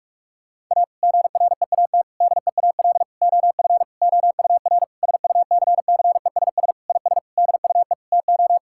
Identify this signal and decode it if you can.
Morse code, heard as a keyed tone, about 45 wpm, 700 Hz. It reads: A GREAT DEAL OF OUR SUCCESS IS DUE TO
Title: Sentences from Top 1000 Words - Review Part 6 - 45wpm